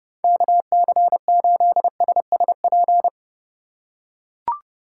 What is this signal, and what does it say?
Morse code, heard as a keyed tone, about 32 wpm, 700 Hz. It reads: KC8SSP
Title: USA Call Signs - 6 Characters - 3x Character Spacing - 30wpm